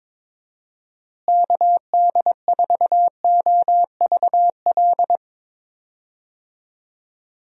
Morse code, heard as a keyed tone, about 22 wpm, 700 Hz. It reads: KD4OVL
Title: USA Call Signs - 6 Characters - Repeated 3x - 22wpm